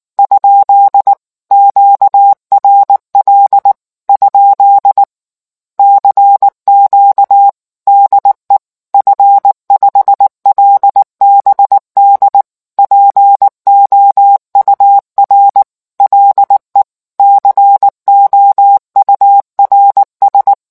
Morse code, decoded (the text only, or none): ? QRL? CQ DE F5LBD POUR LE COURS